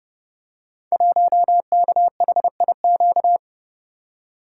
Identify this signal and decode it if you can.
Morse code, heard as a keyed tone, about 30 wpm, 700 Hz. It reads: 1KHIQ
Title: Code Groups - 5 Letter:Numbers - 30wpm